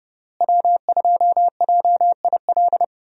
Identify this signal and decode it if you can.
Morse code, heard as a keyed tone, about 31 wpm, 700 Hz. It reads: W2JIL
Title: USA Call Signs - 5 Characters - Repeated 3x - 30wpm